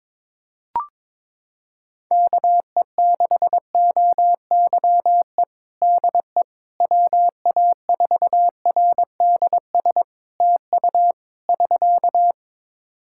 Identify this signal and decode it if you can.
Morse code, heard as a keyed tone, about 21 wpm, 700 Hz. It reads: KE6OYE DE WA4RDS TU <SK>
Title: Practice QSO - Exchange 4 - Ending - 22wpm